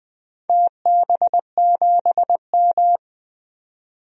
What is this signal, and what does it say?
Morse code, heard as a keyed tone, about 20 wpm, 700 Hz. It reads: TB7M
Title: DXCC-100 Call Signs - 4 Characters - 20wpm